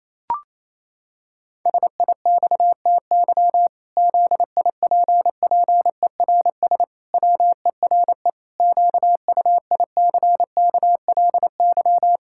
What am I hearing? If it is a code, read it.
Morse code, heard as a keyed tone, about 28 wpm, 700 Hz. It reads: SIXTY ZIPPERS WERE QUICKLY
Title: Pangram Challenge v2 - 28wpm